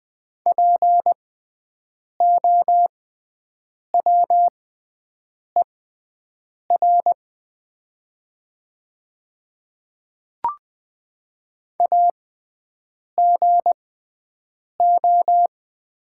Morse code, heard as a keyed tone, about 20 wpm, spaced as for 7 wpm, 700 Hz. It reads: POWER AGO
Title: 6x Character Spacing - Top 500 Words - 20wpm